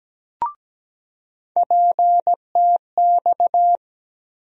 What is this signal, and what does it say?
Morse code, heard as a keyed tone, about 17 wpm, 700 Hz. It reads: PTX
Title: USA Call Sign Postfixes - 3 Letters 17wpm